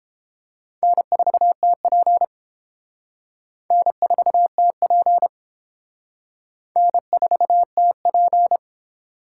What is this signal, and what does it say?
Morse code, heard as a keyed tone, about 31 wpm, 700 Hz. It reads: N4TP N4TP N4TP